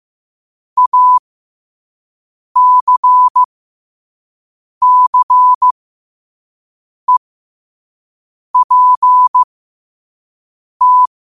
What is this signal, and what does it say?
Morse code, heard as a keyed tone, about 15 wpm, spaced as for 5 wpm, 1000 Hz. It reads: ACCEPT
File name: qotdfiles / 5wpm